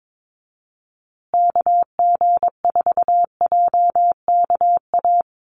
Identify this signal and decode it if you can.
Morse code, heard as a keyed tone, about 22 wpm, 700 Hz. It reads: KG4JKA
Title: USA Call Signs Repeated 2x 22wpm